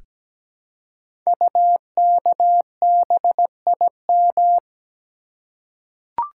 Morse code, heard as a keyed tone, about 17 wpm, 700 Hz. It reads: UKBIM
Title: Code Groups - 5 Letters - 17wpm